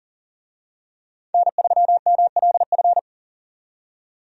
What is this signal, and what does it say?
Morse code, heard as a keyed tone, about 40 wpm, 700 Hz. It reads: N3MLF
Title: USA Call Signs - 5 Characters - 40wpm